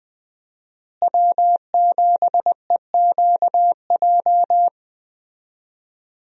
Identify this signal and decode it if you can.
Morse code, heard as a keyed tone, about 20 wpm, 700 Hz. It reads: W7EQJ
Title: USA Call Signs - 5 Characters - Repeated 2x - 20wpm